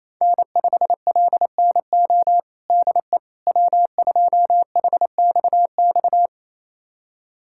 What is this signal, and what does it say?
Morse code, heard as a keyed tone, about 27 wpm, 700 Hz. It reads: N5LNO DE W2HXX
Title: USA Callsign DE Callsign - 5 Characters - 28wpm